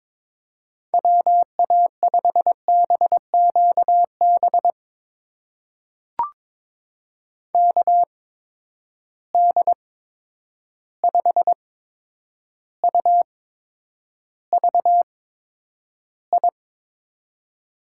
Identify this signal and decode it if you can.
Morse code, heard as a keyed tone, about 22 wpm, spaced as for 9 wpm, 700 Hz. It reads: WA5BQB KD5UVI
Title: USA Call Signs - 6 Characters - 8x Character Spacing - 22wpm